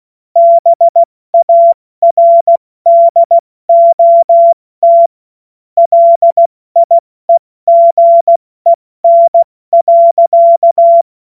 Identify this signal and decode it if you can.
Morse code, heard as a keyed tone, about 16 wpm, 680 Hz. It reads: BARDOT LIEGEN.
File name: Morsekurs / Witze / Witz 04. Januar 2022, 70 F80